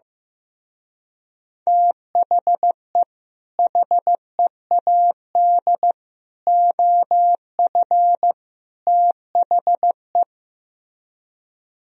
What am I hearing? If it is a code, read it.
Morse code, heard as a keyed tone, about 15 wpm, 700 Hz. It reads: THE HEAD OF THE
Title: Speed-Racing - Sets of 4 Words 15wpm